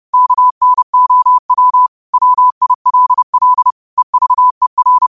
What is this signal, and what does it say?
Morse code, heard as a keyed tone, about 31 wpm, 1000 Hz. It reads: KNOW WILL EVER
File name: qotdfiles / 30wpm